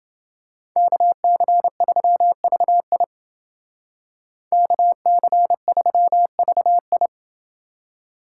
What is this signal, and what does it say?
Morse code, heard as a keyed tone, about 28 wpm, 700 Hz. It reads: KC3VI KC3VI